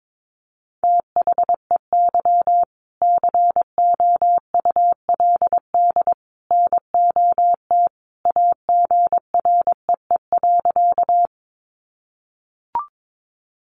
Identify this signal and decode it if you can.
Morse code, heard as a keyed tone, about 22 wpm, 700 Hz. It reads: THEY COULD NOT AGREE.
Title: Sentences from Top 1100 Words 22wpm